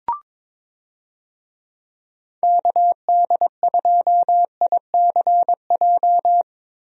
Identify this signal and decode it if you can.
Morse code, heard as a keyed tone, about 22 wpm, 700 Hz. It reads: KD2ICJ
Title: USA Call Signs - 6 Characters - Repeated 3x - 22wpm